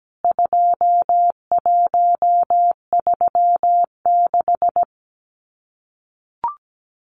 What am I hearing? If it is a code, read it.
Morse code, heard as a keyed tone, about 17 wpm, 700 Hz. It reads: 2136
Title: CWT Exchange 17wpm